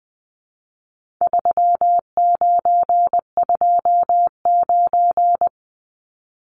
Morse code, heard as a keyed tone, about 20 wpm, 700 Hz. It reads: 3929
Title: Numbers - 4-Digit 20wpm